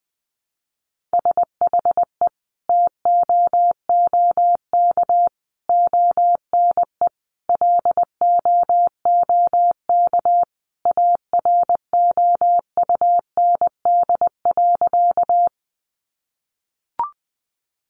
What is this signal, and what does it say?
Morse code, heard as a keyed tone, about 20 wpm, 700 Hz. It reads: SHE TOOK ONE LOOK AROUND.
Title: Sentences from Top 600 Words 20wpm